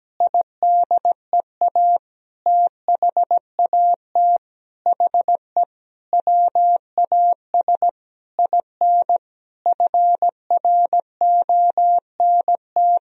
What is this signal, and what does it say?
Morse code, heard as a keyed tone, about 17 wpm, 700 Hz. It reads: IDEA THAT HE WAS IN FRONT
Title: Sentences from Top 1000 Words - Review Part 2 - 17wpm